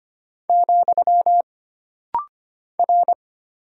Morse code, heard as a keyed tone, about 25 wpm, 700 Hz. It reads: , R
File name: Single Character (Letter:Number:Punctuation:Prosign) - Rapid-Fire - 25wpm